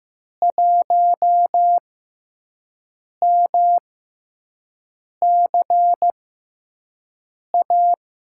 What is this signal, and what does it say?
Morse code, heard as a keyed tone, about 15 wpm, spaced as for 5 wpm, 700 Hz. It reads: 1MCA
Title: USA Call Signs - 6x Character Spacing - 15wpm